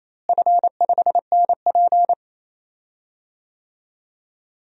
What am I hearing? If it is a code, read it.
Morse code, heard as a keyed tone, about 27 wpm, 700 Hz. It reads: F5NP
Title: DXCC-100 Call Signs - 4 Characters - Repeated 3x - 28wpm